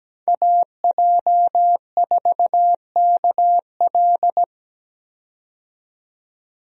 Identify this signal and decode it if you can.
Morse code, heard as a keyed tone, about 17 wpm, 700 Hz. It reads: AJ4KL
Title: USA Call Signs - 5 Characters - 17wpm